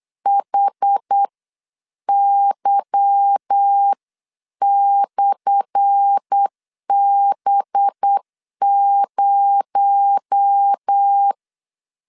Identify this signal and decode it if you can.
Morse code, heard as a keyed tone, about 8 wpm, spaced as for 7 wpm, 800 Hz. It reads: HY/B0
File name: LEZIONI / 10 SIGNS / 10A SLASH 8wpm 3452